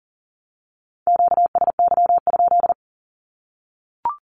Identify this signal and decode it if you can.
Morse code, heard as a keyed tone, about 40 wpm, 700 Hz. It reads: QSY?
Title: Q-Codes 40wpm